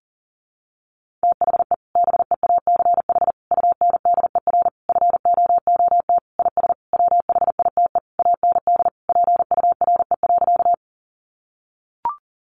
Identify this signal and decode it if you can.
Morse code, heard as a keyed tone, about 40 wpm, 700 Hz. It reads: THE BEACH UNDER FOOT IS WHITE AND PURE.
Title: Sentences from Top 1500 Words 40wpm